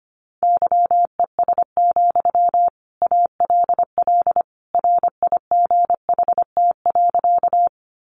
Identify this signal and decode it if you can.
Morse code, heard as a keyed tone, about 25 wpm, 700 Hz. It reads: YES, ALL RIGHT.
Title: Sentences from Top 500 Words 25wpm